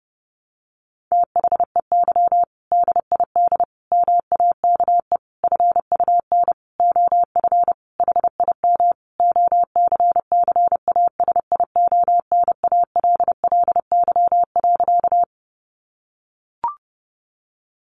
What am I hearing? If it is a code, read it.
Morse code, heard as a keyed tone, about 30 wpm, 700 Hz. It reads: THEY DID MAKE FUN OF HIM OCCASIONALLY.
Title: Sentences from Top 1300 Words 30wpm